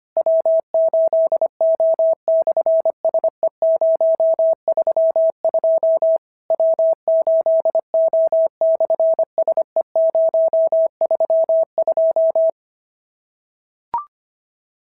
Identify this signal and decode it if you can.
Morse code, heard as a keyed tone, about 25 wpm, 650 Hz. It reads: W8O/SE032 W8O/SE032
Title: SOTA QSO Practice 25wpm